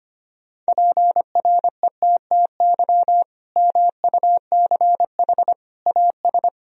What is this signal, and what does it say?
Morse code, heard as a keyed tone, about 25 wpm, 700 Hz. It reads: PRETTY MUCH AS
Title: Sentences from Top 1000 Words - Review Part 2 - 25wpm